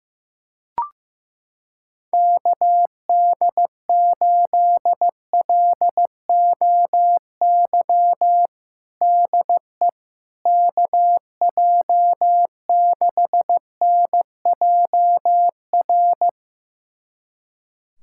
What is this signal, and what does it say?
Morse code, heard as a keyed tone, about 15 wpm, 700 Hz. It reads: KD8LOY DE KJ6NJR
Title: USA Callsign DE Callsign - 6 Characters - 15wpm